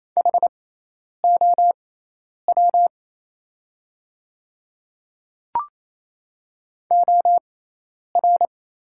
Morse code, heard as a keyed tone, about 27 wpm, spaced as for 10 wpm, 700 Hz. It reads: HOW OR
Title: 6x Character Spacing - Top 200 Words - 28wpm